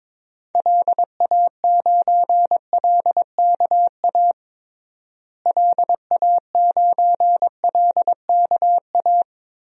Morse code, heard as a keyed tone, about 22 wpm, 700 Hz. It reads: LA9LKA LA9LKA
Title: DXCC-100 Call Signs - 6 Characters - Repeated 2x - 22wpm